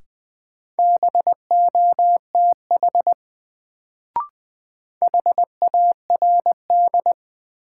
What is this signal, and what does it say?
Morse code, heard as a keyed tone, about 20 wpm, 700 Hz. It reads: BOTH HARD
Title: Sentences from Top 300 Words - New Word Drill - 20wpm